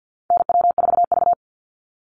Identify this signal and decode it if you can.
Morse code, heard as a keyed tone, about 50 wpm, 700 Hz. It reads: NW4V
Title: Call Signs Repeated 3x - 50wpm